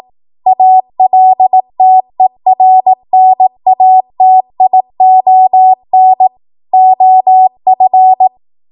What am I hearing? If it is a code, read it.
Morse code, heard as a keyed tone, about 18 wpm, 750 Hz. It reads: ALTERNATION OF
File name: Archive / 18 WPM / 18WPM